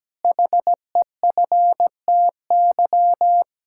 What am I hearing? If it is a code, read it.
Morse code, heard as a keyed tone, about 17 wpm, 700 Hz. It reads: HEFTY
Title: Pangram Challenge v2 - 17wpm